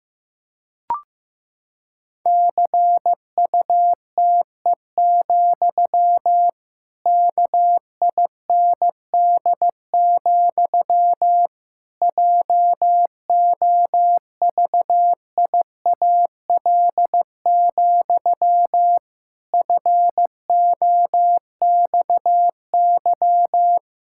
Morse code, heard as a keyed tone, about 15 wpm, 700 Hz. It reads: CUTE, KIND, JOVIAL, FOXY
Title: Pangram Challenge v2 - 15wpm